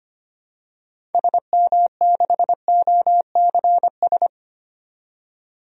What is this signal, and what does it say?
Morse code, heard as a keyed tone, about 25 wpm, 700 Hz. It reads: SM6OCS
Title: DXCC-100 Call Signs - 6 Characters - 25wpm